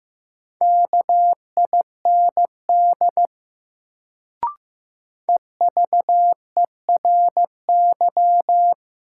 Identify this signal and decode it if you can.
Morse code, heard as a keyed tone, about 15 wpm, 700 Hz. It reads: KIND EVERY
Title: Sentences from Top 200 Words - New Word Drill - 15wpm